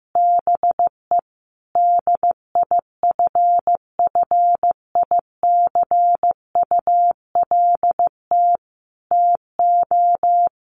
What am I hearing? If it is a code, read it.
Morse code, heard as a keyed tone, about 15 wpm, 700 Hz. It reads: BE DIFFICULT TO
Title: Sentences from Top 1400 Words 15wpm